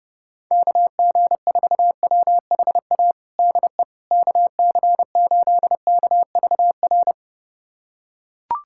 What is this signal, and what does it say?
Morse code, heard as a keyed tone, about 32 wpm, 700 Hz. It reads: KG4WHA DE KC8KVR
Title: USA Callsign DE Callsign - 6 Characters - 30wpm